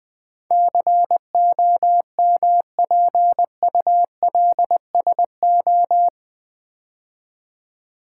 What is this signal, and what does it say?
Morse code, heard as a keyed tone, about 20 wpm, 700 Hz. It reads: COMPULSO
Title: Build A Word - 4 Letters At A Time - 10-Letter Words - 20wpm